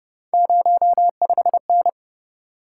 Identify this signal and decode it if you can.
Morse code, heard as a keyed tone, about 32 wpm, 700 Hz. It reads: 05N